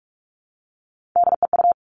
Morse code, heard as a keyed tone, about 46 wpm, 700 Hz. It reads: DEU
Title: Three Challenging Letter-Number Combinations 45wpm